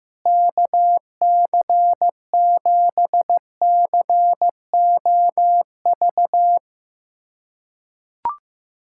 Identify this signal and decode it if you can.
Morse code, heard as a keyed tone, about 15 wpm, 700 Hz. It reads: KC7COV
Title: USA Callsign DE Callsign 15wpm